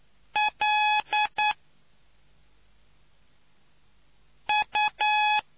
Morse code, heard as a keyed tone, about 9 wpm, 840 Hz. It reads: L U